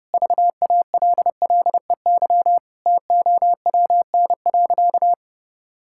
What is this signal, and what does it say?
Morse code, heard as a keyed tone, about 32 wpm, 700 Hz. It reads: VALLEY TOWN.
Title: Pangram Challenge v2 - 30wpm